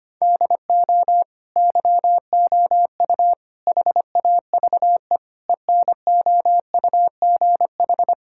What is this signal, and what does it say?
Morse code, heard as a keyed tone, about 25 wpm, 700 Hz. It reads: DO YOU HAVE ENOUGH
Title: Sentences from Top 1000 Words - Review Part 5 - 25wpm